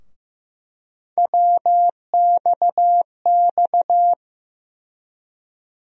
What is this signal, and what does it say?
Morse code, heard as a keyed tone, about 15 wpm, 700 Hz. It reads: WXX
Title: USA Call Sign Postfixes - 3 Letters 15wpm